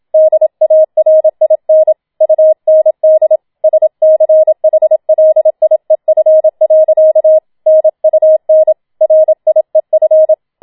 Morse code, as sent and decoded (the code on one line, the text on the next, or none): -.. .- .-. .. -. / ..- -. -.. / ... -.-. .... .-.. .. ..-. .-.-.- / -. ..- -. / .-. .. ..-.
DARIN UND SCHLIEF. NUN RIEF